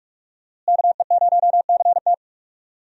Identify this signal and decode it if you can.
Morse code, heard as a keyed tone, about 46 wpm, 700 Hz. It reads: KE0CT
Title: USA Call Signs - 5 Characters - Repeated 3x - 45wpm